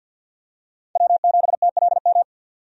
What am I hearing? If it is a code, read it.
Morse code, heard as a keyed tone, about 52 wpm, 700 Hz. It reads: W7TLM
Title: USA Call Signs - 5 Characters - 50wpm